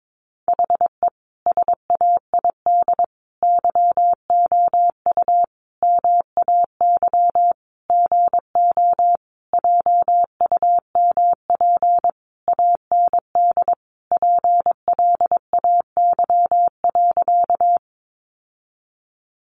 Morse code, heard as a keyed tone, about 22 wpm, 700 Hz. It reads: HE SAID YOU MAY GO JUMP AND PLAY.
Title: Sentences from Top 1300 Words 22wpm